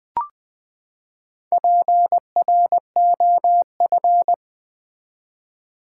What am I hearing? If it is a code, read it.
Morse code, heard as a keyed tone, about 20 wpm, 700 Hz. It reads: PROF
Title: Build A Word - 4 Letters At A Time - 12-Letter Words - 20wpm